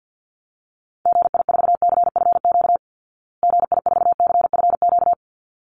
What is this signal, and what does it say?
Morse code, heard as a keyed tone, about 50 wpm, 700 Hz. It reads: GI4CFQ GI4CFQ
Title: Call Signs Repeated 3x - 50wpm